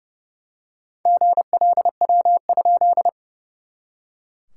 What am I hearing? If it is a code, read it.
Morse code, heard as a keyed tone, about 30 wpm, 700 Hz. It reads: GLW?
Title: SOTA QSO Practice 30wpm